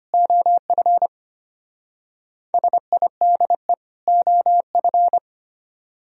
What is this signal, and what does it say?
Morse code, heard as a keyed tone, about 29 wpm, 700 Hz. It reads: OF SIDE OF
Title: Speed-Racing - Sets of 2 Words 25wpm